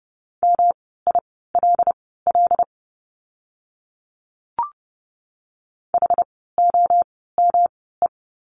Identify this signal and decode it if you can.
Morse code, heard as a keyed tone, about 30 wpm, spaced as for 17 wpm, 700 Hz. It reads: MILL HOME
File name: course / 30:10wpm / 108 Lesson 18f - Letter m - Words 30wpm